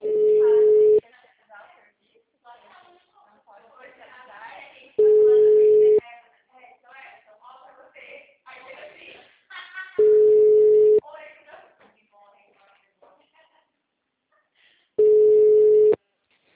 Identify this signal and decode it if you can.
Ringback tone (Europe, 425 Hz), with speech.